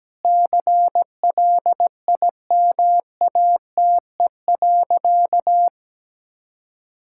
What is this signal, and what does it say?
Morse code, heard as a keyed tone, about 17 wpm, 700 Hz. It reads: CLIMATE.
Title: Sentences from Top 1600 Words 17wpm